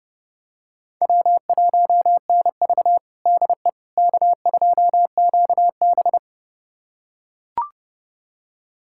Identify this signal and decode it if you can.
Morse code, heard as a keyed tone, about 30 wpm, 700 Hz. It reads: W1NV DE K2QB
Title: USA Callsign DE Callsign - 4 Characters - 30wpm